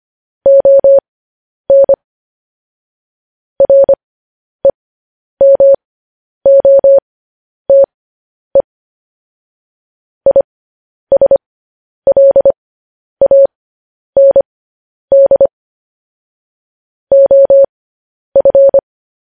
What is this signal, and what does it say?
Morse code, heard as a keyed tone, about 25 wpm, spaced as for 10 wpm, 550 Hz. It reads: ON REMOTE ISLAND OF
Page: news headlines in morse code at 10 wpm